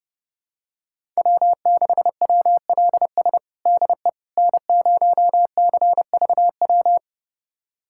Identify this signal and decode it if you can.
Morse code, heard as a keyed tone, about 32 wpm, 700 Hz. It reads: W6WLS DE N0CVW
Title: USA Callsign DE Callsign - 5 Characters - 30wpm